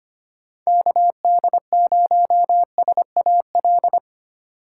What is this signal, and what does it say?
Morse code, heard as a keyed tone, about 25 wpm, 700 Hz. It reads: KD0SAL
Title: USA Call Signs - 6 Characters - 4x Character Spacing - 25wpm